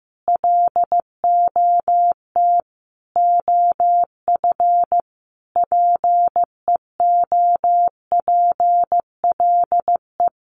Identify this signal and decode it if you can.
Morse code, heard as a keyed tone, about 15 wpm, 700 Hz. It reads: LOT OF PEOPLE
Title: Sets of 5 Words Encore2 15wpm